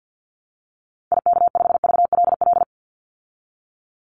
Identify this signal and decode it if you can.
Morse code, heard as a keyed tone, about 50 wpm, 700 Hz. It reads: IK5VLL
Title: Call Signs Repeated 2x - 50wpm